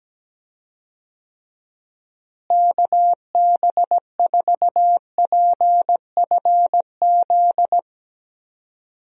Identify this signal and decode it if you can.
Morse code, heard as a keyed tone, about 17 wpm, 700 Hz. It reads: KB4PFZ
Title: USA Call Signs - 6 Characters - Repeated 2x - 17wpm